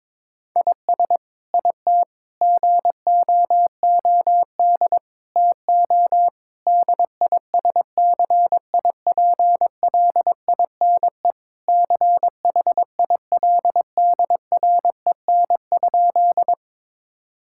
Morse code, heard as a keyed tone, about 22 wpm, 700 Hz. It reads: IS IT GOOD TO DISCIPLINE CHILDREN?
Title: Sentences from Top 1000 Words - Review Part 6 - 22wpm